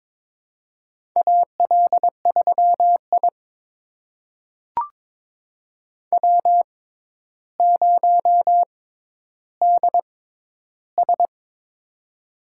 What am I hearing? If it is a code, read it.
Morse code, heard as a keyed tone, about 22 wpm, spaced as for 11 wpm, 700 Hz. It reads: AL3I W0DS
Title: USA Call Signs - 6x Character Spacing - 22wpm